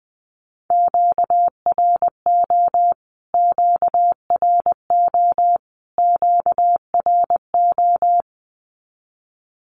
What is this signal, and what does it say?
Morse code, heard as a keyed tone, about 20 wpm, 700 Hz. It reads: QRO QRO QRO
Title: Q-Codes Repeated 3x 20wpm